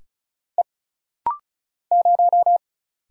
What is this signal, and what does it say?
Morse code, heard as a keyed tone, about 34 wpm, 700 Hz. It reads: E 0
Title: Single Letter-Number Alt - Rapid-Fire 35wpm